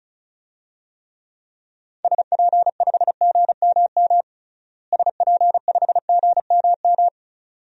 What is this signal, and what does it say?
Morse code, heard as a keyed tone, about 34 wpm, 700 Hz. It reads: SP5GMM SP5GMM
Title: DXCC-100 Call Signs - 6 Characters - Repeated 2x - 35wpm